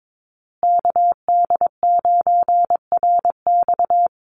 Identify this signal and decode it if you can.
Morse code, heard as a keyed tone, about 22 wpm, 700 Hz. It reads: KD9RX
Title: USA Call Signs Repeated 3x 22wpm